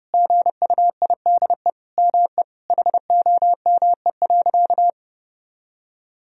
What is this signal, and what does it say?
Morse code, heard as a keyed tone, about 30 wpm, 700 Hz. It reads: GUIDE ME HOME.
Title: Sentences from Top 1000 Words - Review Part 3 - 30wpm